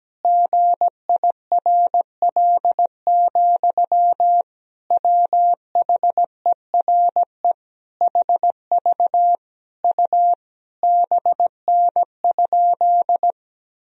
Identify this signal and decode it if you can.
Morse code, heard as a keyed tone, about 17 wpm, 700 Hz. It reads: GIRL, WHERE HV U BN?
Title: Sentences from Top 400 Words and Abbreviations 17wpm